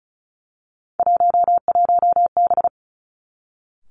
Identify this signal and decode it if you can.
Morse code, heard as a keyed tone, about 36 wpm, 700 Hz. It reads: 11B
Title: Three Challenging Letter-Number Combinations 35wpm